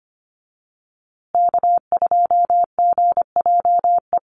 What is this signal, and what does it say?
Morse code, heard as a keyed tone, about 25 wpm, 700 Hz. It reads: K2GJE
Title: USA Call Signs Repeated 2x 25wpm